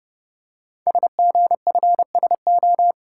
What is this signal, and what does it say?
Morse code, heard as a keyed tone, about 30 wpm, 700 Hz. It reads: SGFSO
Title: Code Groups - 5 Letters - 30wpm